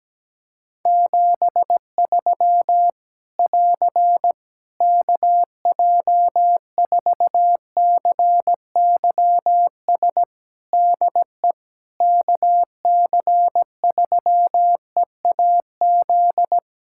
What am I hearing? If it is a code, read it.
Morse code, heard as a keyed tone, about 17 wpm, 700 Hz. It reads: 73 <AR> KJ4CYS DE KC3EAZ
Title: Practice QSO - Exchange 4 - Ending - 17wpm